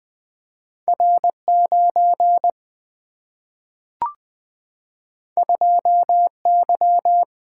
Morse code, heard as a keyed tone, about 20 wpm, 700 Hz. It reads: R9 2Y
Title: Single-Letter Single-Number 20wpm